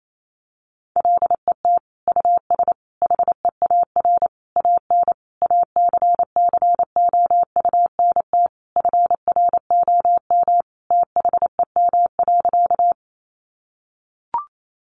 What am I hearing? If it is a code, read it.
Morse code, heard as a keyed tone, about 28 wpm, 700 Hz. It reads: LET US HEAR AN ACCOUNT FROM THEM.
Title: Sentences from Top 900 Words 28wpm